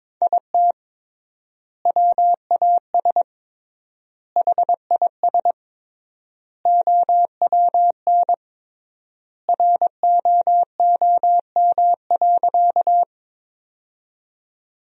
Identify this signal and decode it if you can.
Morse code, heard as a keyed tone, about 22 wpm, 700 Hz. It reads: IT WAS HIS OWN ROOM.
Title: 3x Word-Spacing - Sentences from Top 300 Words - 22wpm